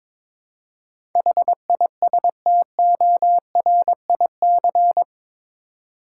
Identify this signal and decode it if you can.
Morse code, heard as a keyed tone, about 22 wpm, 700 Hz. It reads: HISTORIC
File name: practice / 22wpm / Eight Letter Words - 22wpm